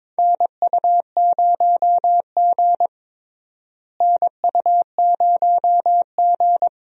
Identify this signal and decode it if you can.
Morse code, heard as a keyed tone, about 21 wpm, 700 Hz. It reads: NU0G NU0G
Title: USA Call Signs - 4 Characters - Repeated 2x - 22wpm